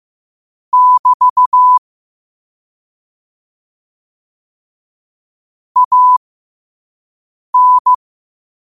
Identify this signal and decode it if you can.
Morse code, heard as a keyed tone, about 15 wpm, spaced as for 5 wpm, 1000 Hz. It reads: <BT> AN